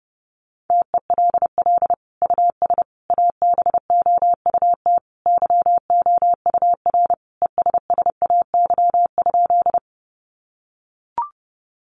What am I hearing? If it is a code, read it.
Morse code, heard as a keyed tone, about 30 wpm, 700 Hz. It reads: TELL US ABOUT YOUR ESSAY?
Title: Sentences from Top 1500 Words 30wpm